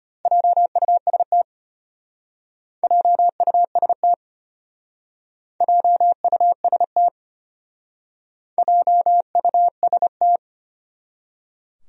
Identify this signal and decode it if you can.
Morse code, heard as a keyed tone, about 30 wpm, 700 Hz. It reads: JUST JUST JUST JUST
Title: Speed-Racing Top 200 Words 25wpm